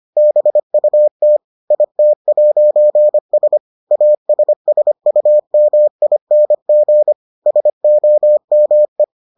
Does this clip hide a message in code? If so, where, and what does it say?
Yes: Morse code from 0.2 to 9.0 s, heard as a keyed tone, about 25 wpm, 600 Hz. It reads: BUT IT'S ASSUMING SOME